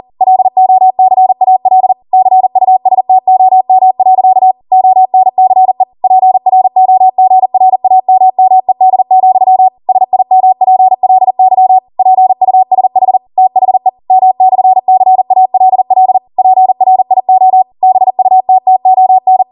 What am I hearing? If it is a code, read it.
Morse code, heard as a keyed tone, about 40 wpm, 750 Hz. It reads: LOCAL CUSTOM. ONCE PROGRAMMED, SIMPLY PUSH THE M/CALL PRIO BUTTON
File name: Morse / 40WPM